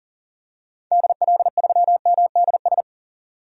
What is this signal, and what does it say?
Morse code, heard as a keyed tone, about 40 wpm, 700 Hz. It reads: DL3MDS
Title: DXCC-100 Call Signs - 6 Characters - 40wpm